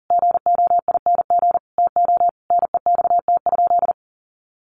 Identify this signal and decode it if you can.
Morse code, heard as a keyed tone, about 37 wpm, 700 Hz. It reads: GOING TO NEXT?
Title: Sentences from Top 1200 Words 40wpm